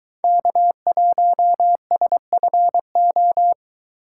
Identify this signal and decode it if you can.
Morse code, heard as a keyed tone, about 23 wpm, 700 Hz. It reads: K1SFO